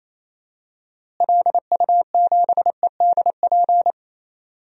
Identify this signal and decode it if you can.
Morse code, heard as a keyed tone, about 28 wpm, 700 Hz. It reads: LU7EDP
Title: DXCC-100 Call Signs - 6 Characters - Repeated 2x - 28wpm